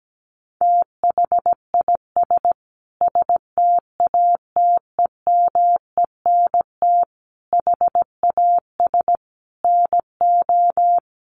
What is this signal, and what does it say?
Morse code, heard as a keyed tone, about 17 wpm, 700 Hz. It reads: THIS STATEMENT HAS NO
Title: Sentences from Top 1200 Words 17wpm